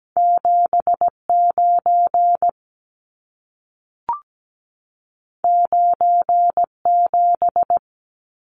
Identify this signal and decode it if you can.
Morse code, heard as a keyed tone, about 17 wpm, 700 Hz. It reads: 79 97
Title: Numbers - 2-Digit 17wpm